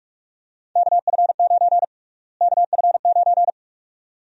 Morse code, heard as a keyed tone, about 45 wpm, 700 Hz. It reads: KF9 KF9
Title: USA Call Sign Prefixes - Repeated 2x - 45wpm